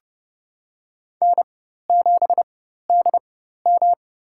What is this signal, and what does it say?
Morse code, heard as a keyed tone, about 30 wpm, spaced as for 14 wpm, 700 Hz. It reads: N7DM
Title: USA Call Signs - 4 Characters - 4x Character Spacing - 30wpm